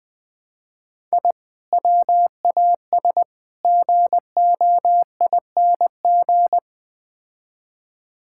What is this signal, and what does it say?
Morse code, heard as a keyed tone, about 20 wpm, 700 Hz. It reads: I WAS GOING
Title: Build A Phrase - 4-Word Phrases - 20wpm